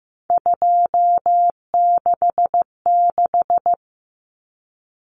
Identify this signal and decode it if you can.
Morse code, heard as a keyed tone, about 15 wpm, 700 Hz. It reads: 266
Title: Common Cut Numbers - 3-Digit 15wpm